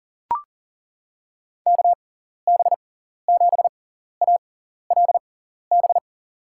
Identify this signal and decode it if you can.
Morse code, heard as a keyed tone, about 38 wpm, spaced as for 14 wpm, 700 Hz. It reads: KB7ALB
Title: USA Call Signs - 6x Character Spacing - 40wpm